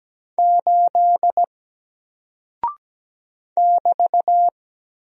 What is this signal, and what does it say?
Morse code, heard as a keyed tone, about 17 wpm, 700 Hz. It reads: 8 <BT>
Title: Single Character (Letter:Number:Punctuation:Prosign) Alt 17wpm